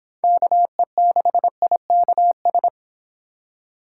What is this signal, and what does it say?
Morse code, heard as a keyed tone, about 26 wpm, 700 Hz. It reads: KE6IKS